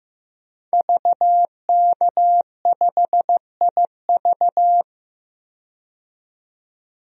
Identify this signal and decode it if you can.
Morse code, heard as a keyed tone, about 15 wpm, 700 Hz. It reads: VK5IV
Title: DXCC-100 Call Signs - 5 Characters - 15wpm